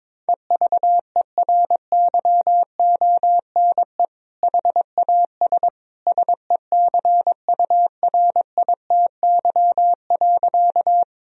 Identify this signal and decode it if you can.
Morse code, heard as a keyed tone, about 22 wpm, 700 Hz. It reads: EVERYONE HAS SECURITY.
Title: Sentences from Top 1000 Words - Review Part 4 - 22wpm